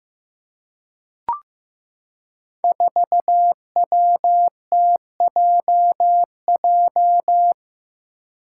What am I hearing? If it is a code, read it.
Morse code, heard as a keyed tone, about 15 wpm, 700 Hz. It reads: 4WTJJ
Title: Five Challenging Letter-Number Combinations 15wpm